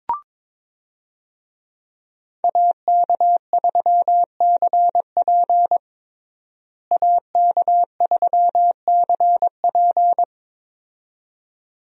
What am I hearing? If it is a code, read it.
Morse code, heard as a keyed tone, about 22 wpm, 700 Hz. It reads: AK3CP AK3CP
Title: USA Call Signs - 5 Characters - Repeated 2x - 22wpm